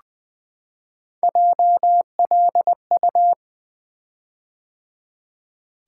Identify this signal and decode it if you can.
Morse code, heard as a keyed tone, about 20 wpm, 700 Hz. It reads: JLU
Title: Three Letter-Number Combinations - 20wpm